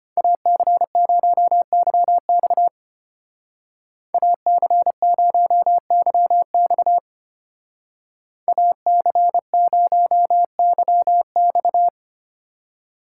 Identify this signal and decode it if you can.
Morse code, heard as a keyed tone, about 31 wpm, 700 Hz. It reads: AC0YX AC0YX AC0YX
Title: Speed Racing - USA 2x2 Call Signs 25wpm